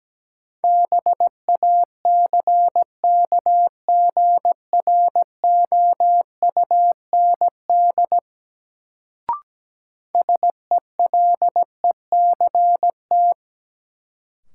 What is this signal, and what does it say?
Morse code, heard as a keyed tone, about 17 wpm, 700 Hz. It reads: BACKGROUND SELECT
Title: Sentences from Top 1300 Words - New Word Drill - 17wpm